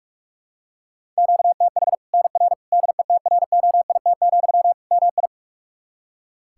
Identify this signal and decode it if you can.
Morse code, heard as a keyed tone, about 45 wpm, 700 Hz. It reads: QTH NR DETROIT, MI
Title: Practice QSO - Exchange 1 - Beginning - 45wpm